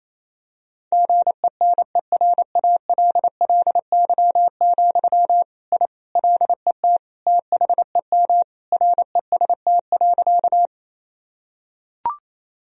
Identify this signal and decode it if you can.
Morse code, heard as a keyed tone, about 27 wpm, 700 Hz. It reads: GENERALLY, I LET THEM REST.
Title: Sentences from Top 1000 Words - Review Part 4 - 28wpm